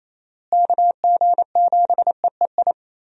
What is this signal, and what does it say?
Morse code, heard as a keyed tone, about 28 wpm, 700 Hz. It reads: KG7EEI
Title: USA Call Signs - 6 Characters - Repeated 2x - 28wpm